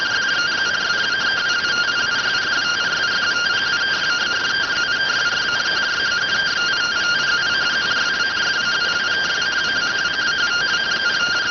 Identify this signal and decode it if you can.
Two-tone teleprinter signal, about 170 Hz shift.